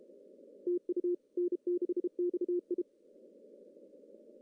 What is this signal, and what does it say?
Morse code, heard as a keyed tone, about 32 wpm, 350 Hz. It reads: TU N6XI
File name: CW / cqww-cw-k5tr